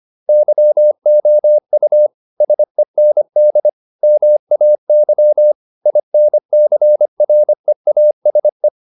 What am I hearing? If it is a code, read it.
Morse code, heard as a keyed tone, about 25 wpm, 600 Hz. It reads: YOU SEND MAY INCREASE